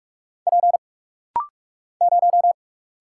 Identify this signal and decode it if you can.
Morse code, heard as a keyed tone, about 46 wpm, 700 Hz. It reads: P 0
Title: Single Letter-Number - Rapid-Fire 45wpm